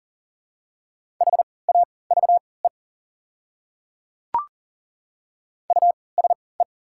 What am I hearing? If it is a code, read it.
Morse code, heard as a keyed tone, about 38 wpm, spaced as for 22 wpm, 690 Hz. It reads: HAVE USE
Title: 3x Character Spacing - Top 100 Words - 40wpm